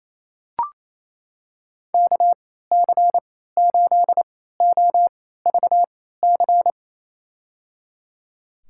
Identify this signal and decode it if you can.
Morse code, heard as a keyed tone, about 28 wpm, spaced as for 16 wpm, 700 Hz. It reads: KC8OVC
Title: USA Call Signs - 6 Characters - 3x Character Spacing - 28wpm